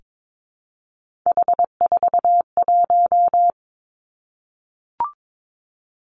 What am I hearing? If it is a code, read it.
Morse code, heard as a keyed tone, about 22 wpm, 700 Hz. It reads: H41
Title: Three Challenging Letter-Number Combinations 22wpm